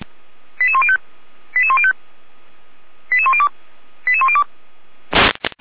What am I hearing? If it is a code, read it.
Five-tone selective call (ZVEI), heard as quick runs of five tones, tones of about 70 ms. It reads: 88105 88105 88102 88102